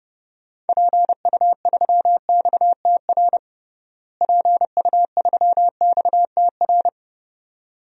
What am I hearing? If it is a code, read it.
Morse code, heard as a keyed tone, about 30 wpm, 700 Hz. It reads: PU3XTR PU3XTR
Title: DXCC-100 Call Signs - 6 Characters - Repeated 2x - 30wpm